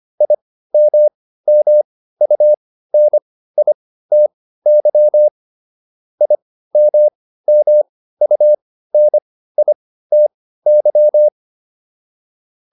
Morse code, heard as a keyed tone, about 25 wpm, spaced as for 15 wpm, 600 Hz. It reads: IMMUNITY IMMUNITY